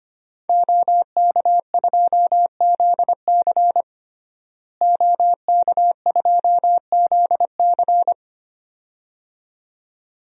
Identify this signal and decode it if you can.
Morse code, heard as a keyed tone, about 25 wpm, 700 Hz. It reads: OK2ZC OK2ZC
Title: DXCC-100 Call Signs - 5 Characters - Repeated 2x - 25wpm